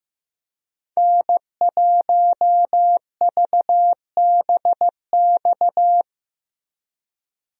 Morse code, heard as a keyed tone, about 15 wpm, 700 Hz. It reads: N1VBX
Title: USA Call Signs - 5 Characters - Repeated 2x - 15wpm